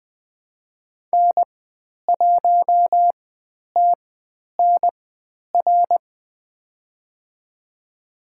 Morse code, heard as a keyed tone, about 20 wpm, spaced as for 10 wpm, 700 Hz. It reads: N1TNR
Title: 095 Lesson 16g - Letter c - Call Signs 20wpm